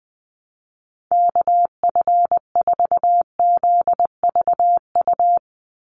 Morse code, heard as a keyed tone, about 20 wpm, 700 Hz. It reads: KF4ZVU
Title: USA Call Signs 20wpm